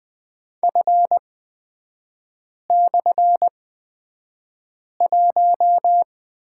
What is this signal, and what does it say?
Morse code, heard as a keyed tone, about 20 wpm, spaced as for 5 wpm, 700 Hz. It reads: F/1
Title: 229 Lesson 37e - Letter x - Three Characters 20wpm